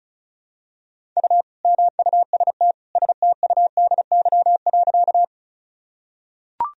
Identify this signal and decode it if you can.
Morse code, heard as a keyed tone, about 36 wpm, 700 Hz. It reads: U MUST STUDY.
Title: Sentences from Top 300 Words and Abbreviations 35wpm